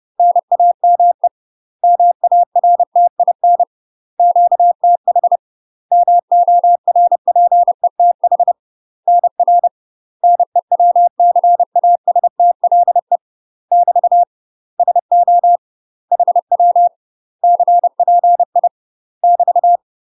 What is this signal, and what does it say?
Morse code, heard as a keyed tone, about 30 wpm, 700 Hz. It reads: NAME MARTIN QTH MORPETH NR NEWCASTLE <BT> SO HW CPI <BT>
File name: UK MASTER QSOs TRAINING / 30WPM / UK MASTER QSOs 0038